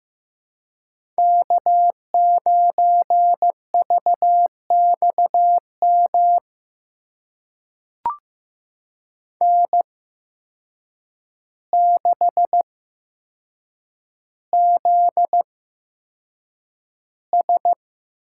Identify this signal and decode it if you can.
Morse code, heard as a keyed tone, about 15 wpm, spaced as for 7 wpm, 700 Hz. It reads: K9VXM N6ZS
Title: USA Call Signs - 8x Character Spacing - 15wpm